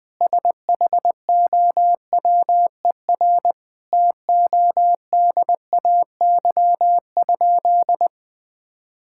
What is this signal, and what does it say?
Morse code, heard as a keyed tone, about 20 wpm, 700 Hz. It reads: SHOWER TODAY?
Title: Sentences from Top 1600 Words 20wpm